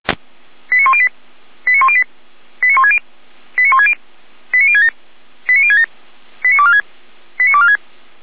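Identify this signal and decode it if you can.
Five-tone selective call (ZVEI), heard as quick runs of five tones, tones of about 70 ms. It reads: 89108 89108 89160 89160 89067 89067 89246 89246
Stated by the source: ZVEI five-tone call 89560 received by SDR.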